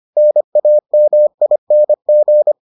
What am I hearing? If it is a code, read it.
Morse code, heard as a keyed tone, about 25 wpm, 600 Hz. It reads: NAMING